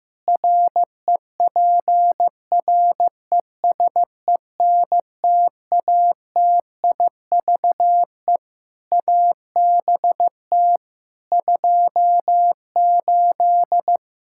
Morse code, heard as a keyed tone, about 15 wpm, 700 Hz. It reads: REPRESENTATIVE ABT 28
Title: Practice QSO - Exchange 3 - More Info - 15wpm